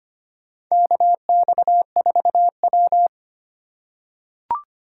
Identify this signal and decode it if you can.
Morse code, heard as a keyed tone, about 25 wpm, 700 Hz. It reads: KX4W
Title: USA Call Signs - 8x Character Spacing - 25wpm